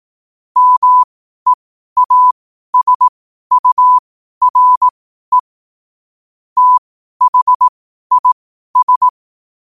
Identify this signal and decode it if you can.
Morse code, heard as a keyed tone, about 18 wpm, spaced as for 12 wpm, 1000 Hz. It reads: MEASURE THIS
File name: qotdfiles / 12wpm